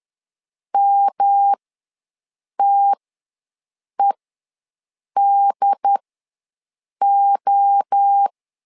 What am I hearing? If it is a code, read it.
Morse code, heard as a keyed tone, about 10 wpm, spaced as for 6 wpm, 800 Hz. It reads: MTEDO
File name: LEZIONI / 3 MWRO / 3 MWRO ws 10wpm 2310